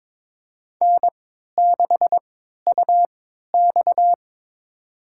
Morse code, heard as a keyed tone, about 23 wpm, spaced as for 13 wpm, 700 Hz. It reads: N6UX
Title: USA Call Signs - 4 Characters - 3x Character Spacing - 22wpm